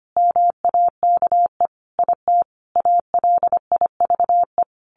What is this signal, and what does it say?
Morse code, heard as a keyed tone, about 25 wpm, 700 Hz. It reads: MAKE IT ALIVE
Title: A Sound Of Thunder 25wpm